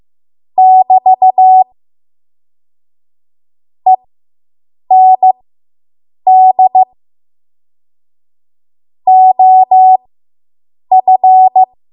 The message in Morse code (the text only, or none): <BT> END OF